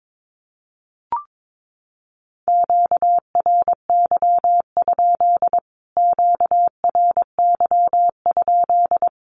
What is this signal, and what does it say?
Morse code, heard as a keyed tone, about 22 wpm, 700 Hz. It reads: QRY? QRY?
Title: Q-Codes Repeated 3x 22wpm